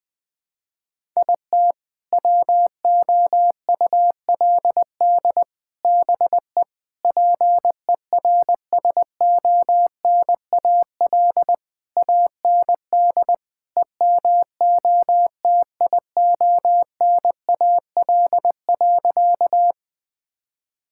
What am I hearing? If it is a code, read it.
Morse code, heard as a keyed tone, about 20 wpm, 700 Hz. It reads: IT WOULD BE PERSONAL AND EMOTIONAL.
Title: Sentences from Top 1000 Words - Review Part 5 - 20wpm